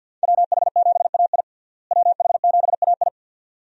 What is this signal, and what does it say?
Morse code, heard as a keyed tone, about 52 wpm, 700 Hz. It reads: WH7AI WH7AI
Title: USA Call Signs - 5 Characters - Repeated 2x - 50wpm